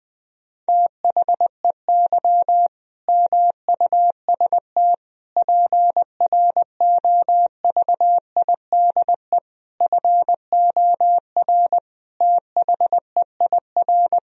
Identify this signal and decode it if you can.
Morse code, heard as a keyed tone, about 20 wpm, 700 Hz. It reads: THEY MUST PROVIDE FOR THEIR
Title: Sentences from Top 1000 Words - Review Part 3 - 20wpm